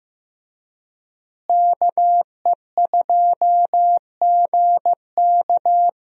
Morse code, heard as a keyed tone, about 16 wpm, 700 Hz. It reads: KE2GK
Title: USA Call Signs - 5 Characters - Repeated 3x - 15wpm